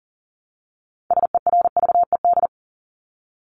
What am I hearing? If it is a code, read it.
Morse code, heard as a keyed tone, about 40 wpm, 700 Hz. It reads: SERVED
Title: ED Suffix - Top 100 Words with Base Words 40wpm